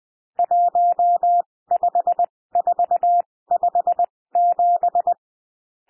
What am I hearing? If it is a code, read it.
Morse code, heard as a keyed tone, about 20 wpm, spaced as for 16 wpm, 700 Hz. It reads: 15457
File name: cwmp3 / N1 SOUND NUMBERS 20 wpm